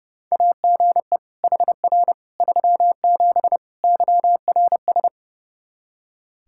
Morse code, heard as a keyed tone, about 32 wpm, 700 Hz. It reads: AGE HR 37 YRS
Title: Practice QSO - Exchange 3 - More Info - 30wpm